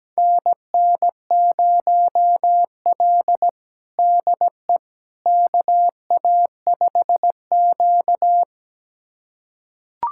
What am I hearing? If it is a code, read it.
Morse code, heard as a keyed tone, about 17 wpm, 700 Hz. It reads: NN0L DE KA5Q
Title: USA Callsign DE Callsign - 4 Characters - 17wpm